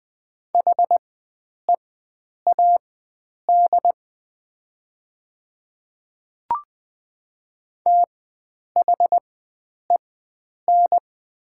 Morse code, heard as a keyed tone, about 20 wpm, spaced as for 9 wpm, 700 Hz. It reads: HEAD THEN
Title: 4x Character Spacing - Top 300 Words - 20wpm